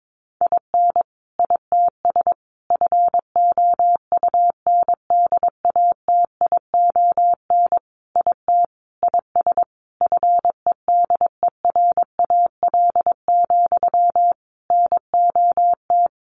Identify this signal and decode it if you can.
Morse code, heard as a keyed tone, about 22 wpm, 700 Hz. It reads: IN ITS FOUNDATION IT IS FEDERAL, NOT
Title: Sentences from Top 1200 Words 22wpm